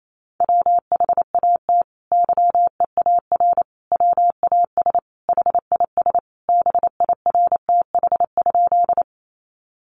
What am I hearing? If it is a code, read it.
Morse code, heard as a keyed tone, about 28 wpm, 700 Hz. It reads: WHAT YEAR WAS HIS BIRTH?
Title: Sentences from Top 800 Words 28wpm